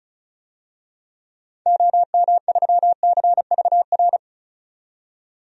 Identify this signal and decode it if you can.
Morse code, heard as a keyed tone, about 36 wpm, 700 Hz. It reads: OM3CVR
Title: DXCC-100 Call Signs - 6 Characters - 35wpm